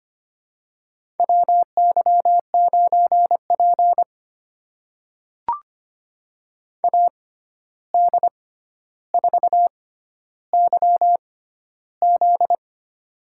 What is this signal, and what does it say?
Morse code, heard as a keyed tone, about 25 wpm, spaced as for 12 wpm, 700 Hz. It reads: WY9P AD4YZ
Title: USA Call Signs - 6x Character Spacing - 25wpm